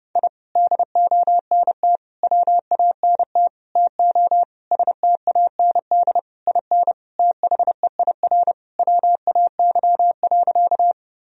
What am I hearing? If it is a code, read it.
Morse code, heard as a keyed tone, about 32 wpm, 700 Hz. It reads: I DONT WANT TO STAND IN THEIR WAY.
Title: Sentences from Top 1000 Words - Review Part 1 - 30wpm